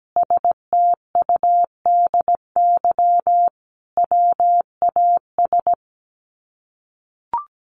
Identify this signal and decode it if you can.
Morse code, heard as a keyed tone, about 17 wpm, 700 Hz. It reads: STUDY WAS
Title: Sets of 5 Words 17wpm